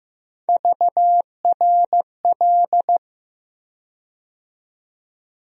Morse code, heard as a keyed tone, about 15 wpm, 700 Hz. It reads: VRL
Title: USA Call Sign Postfixes - 3 Letters 15wpm